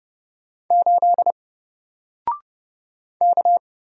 Morse code, heard as a keyed tone, about 30 wpm, 700 Hz. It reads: 8 K
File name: Single-Letter-Number-30wpm